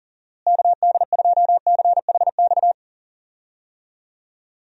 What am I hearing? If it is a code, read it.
Morse code, heard as a keyed tone, about 40 wpm, 700 Hz. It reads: KD2CHX
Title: USA Call Signs - 6 Characters - Repeated 2x - 40wpm